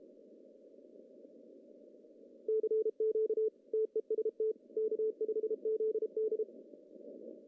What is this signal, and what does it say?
Morse code, heard as a keyed tone, about 31 wpm, 440 Hz. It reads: CQ TEST K5ZD
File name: CW / cqww-cw-k5tr